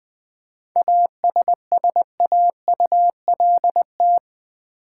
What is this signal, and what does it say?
Morse code, heard as a keyed tone, about 20 wpm, 700 Hz. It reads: ASSAULT
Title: 7 Letter Words - First 3 Letters Given - 20wpm